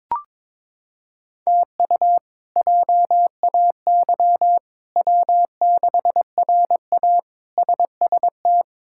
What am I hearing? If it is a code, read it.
Morse code, heard as a keyed tone, about 21 wpm, 700 Hz. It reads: TU JAY W6RA SST
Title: K1USN SST QSO Practice - 22wpm